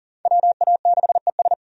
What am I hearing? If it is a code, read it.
Morse code, heard as a keyed tone, about 40 wpm, 700 Hz. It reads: WA6ES
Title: USA Call Signs - 5 Characters - 40wpm